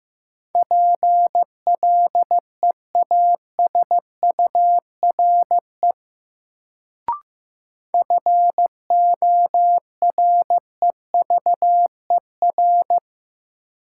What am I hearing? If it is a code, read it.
Morse code, heard as a keyed tone, about 15 wpm, 700 Hz. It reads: PLEASURE FOREVER
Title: Sentences from Top 1600 Words - New Word Drill - 15wpm